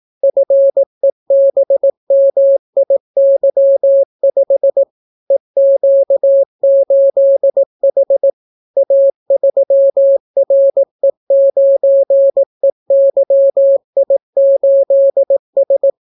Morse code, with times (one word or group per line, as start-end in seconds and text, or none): 0.2-4.8 FEBMIY5
5.3-8.3 EQ8H
8.8-15.9 A3RE9EYI8S